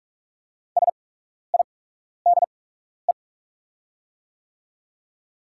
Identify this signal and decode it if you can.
Morse code, heard as a keyed tone, about 43 wpm, spaced as for 12 wpm, 700 Hz. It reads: SIDE
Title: 8x Character Spacing - Top 200 Words - 45wpm